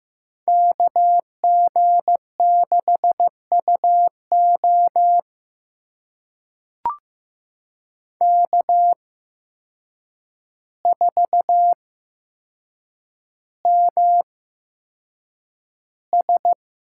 Morse code, heard as a keyed tone, about 15 wpm, spaced as for 7 wpm, 700 Hz. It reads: KG6UO K4MS
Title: USA Call Signs - 5 Characters - 8x Character Spacing - 15wpm